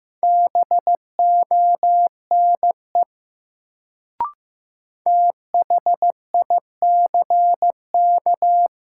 Morse code, heard as a keyed tone, about 15 wpm, 700 Hz. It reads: BONE THICK
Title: Sentences from Top 1600 Words - New Word Drill Alt - 15wpm